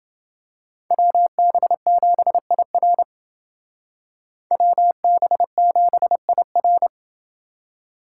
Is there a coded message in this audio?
Yes: Morse code from 0.9 to 6.9 s, heard as a keyed tone, about 29 wpm, 700 Hz. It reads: WB7IR WB7IR